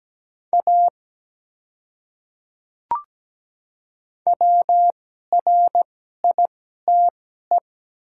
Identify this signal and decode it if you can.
Morse code, heard as a keyed tone, about 17 wpm, spaced as for 12 wpm, 700 Hz. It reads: A WRITE
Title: 2x Character Spacing - Top 100 Words - 17wpm